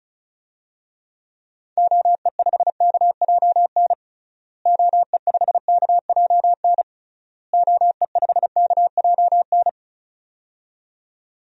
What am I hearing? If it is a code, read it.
Morse code, heard as a keyed tone, about 34 wpm, 700 Hz. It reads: OE5KJN OE5KJN OE5KJN
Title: DXCC-100 Call Signs - 6 Characters - Repeated 3x - 35wpm